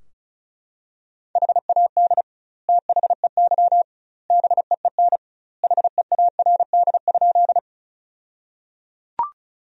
Morse code, heard as a keyed tone, about 36 wpm, 700 Hz. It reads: HAD THEY BEEN HEARD?
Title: 2x Word-Spacing - Sentences from Top 400 Words - 35wpm